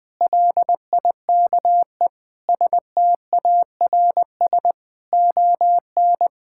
Morse code, heard as a keyed tone, about 20 wpm, 700 Hz. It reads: LIKE STARS ON
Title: Difficult Sentences with Hints - 20wpm